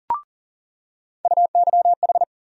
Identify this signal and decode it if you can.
Morse code, heard as a keyed tone, about 40 wpm, 700 Hz. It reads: UYH
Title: USA Call Sign Postfixes - 3 Letters 40wpm